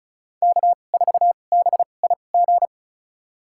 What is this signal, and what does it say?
Morse code, heard as a keyed tone, about 34 wpm, spaced as for 25 wpm, 700 Hz. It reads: K4BIG
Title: USA Call Signs - 5 Characters - 2x Character Spacing - 35wpm